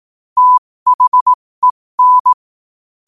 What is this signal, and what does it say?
Morse code, heard as a keyed tone, about 18 wpm, spaced as for 15 wpm, 1000 Hz. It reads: THEN
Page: qotd-at-15-wpm